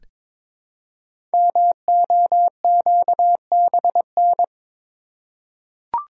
Morse code, heard as a keyed tone, about 22 wpm, 700 Hz. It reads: MOQBN
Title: Code Groups - 5 Letters - 22wpm